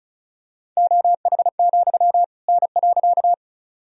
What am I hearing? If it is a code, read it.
Morse code, heard as a keyed tone, about 36 wpm, 700 Hz. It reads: OH, N.
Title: Sentences from Top 500 Words and Abbreviations 35wpm